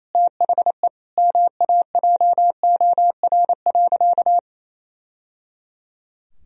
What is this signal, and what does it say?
Morse code, heard as a keyed tone, about 28 wpm, 700 Hz. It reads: THE MAJOR.
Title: Sentences from Top 1000 Words - Review Part 4 - 28wpm